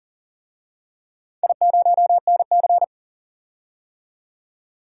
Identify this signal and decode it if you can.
Morse code, heard as a keyed tone, about 40 wpm, 700 Hz. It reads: I0NC
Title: DXCC-100 Call Signs - 4 Characters - 40wpm